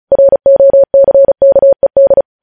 Morse code, heard as a keyed tone, about 36 wpm, 550 Hz. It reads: ROCKED
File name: podcast / News headlines-1x-35x35